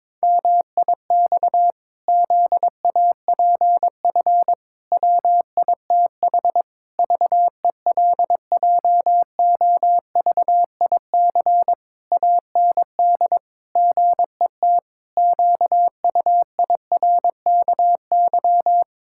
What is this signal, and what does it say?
Morse code, heard as a keyed tone, about 22 wpm, 700 Hz. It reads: MIX ZAPF WITH VELJOVIC AND GET QUIRKY
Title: Pangram Challenge v2 - 22wpm